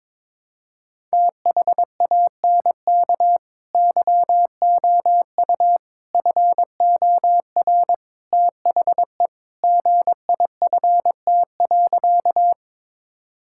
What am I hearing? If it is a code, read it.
Morse code, heard as a keyed tone, about 22 wpm, 700 Hz. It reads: THANK YOU FOR THE GIFT.
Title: Sentences from Top 1000 Words - Review Part 6 - 22wpm